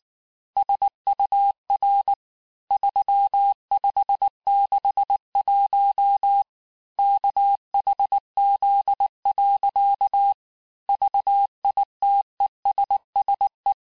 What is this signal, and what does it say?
Morse code, heard as a keyed tone, about 18 wpm, 800 Hz. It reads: SUR 3561 KHZ. VITESSE
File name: cours-du-mercredi / audios / cours-869-texte-CHIFFRE-vitesse-18-MPM